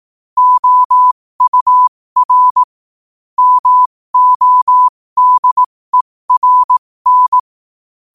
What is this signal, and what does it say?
Morse code, heard as a keyed tone, about 18 wpm, spaced as for 15 wpm, 1000 Hz. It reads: OUR MODERN
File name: qotdfiles / 15wpm